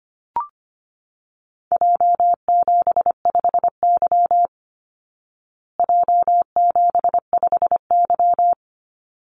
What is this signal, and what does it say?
Morse code, heard as a keyed tone, about 25 wpm, 700 Hz. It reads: J75Y J75Y
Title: Call Signs Repeated 3x - 25wpm